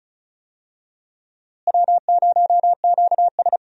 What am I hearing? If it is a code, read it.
Morse code, heard as a keyed tone, about 36 wpm, 700 Hz. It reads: W0QS
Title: USA Call Signs - 4 Characters - Repeated 3x - 35wpm